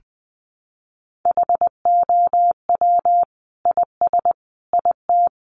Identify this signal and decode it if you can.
Morse code, heard as a keyed tone, about 20 wpm, 700 Hz. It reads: HOW IS IT
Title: Sentences from Top 900 Words 20wpm